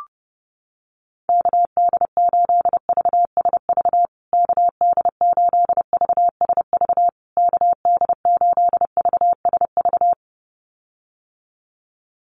Morse code, heard as a keyed tone, about 28 wpm, 700 Hz. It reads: KD8VSV KD8VSV KD8VSV
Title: USA Call Signs Repeated 3x 30wpm